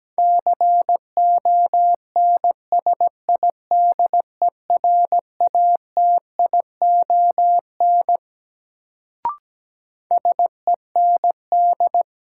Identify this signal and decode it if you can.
Morse code, heard as a keyed tone, about 17 wpm, 700 Hz. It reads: CONSIDERATION SEND
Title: Sentences from Top 1500 Words - New Word Drill Alt - 17wpm